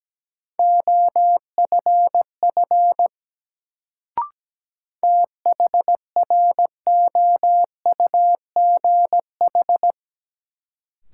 Morse code, heard as a keyed tone, about 17 wpm, 700 Hz. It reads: OFF THROUGH
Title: Sentences from Top 200 Words - New Word Drill - 17wpm